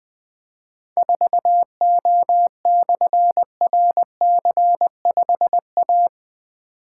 Morse code, heard as a keyed tone, about 21 wpm, 700 Hz. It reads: 4O/RC5A
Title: Difficult Call Signs 20wpm